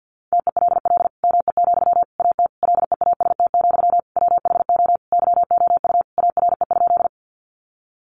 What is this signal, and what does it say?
Morse code, heard as a keyed tone, about 50 wpm, 700 Hz. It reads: TELL ME, AT LEAST, WHO YOU ARE?
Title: Sentences from Top 700 Words 50wpm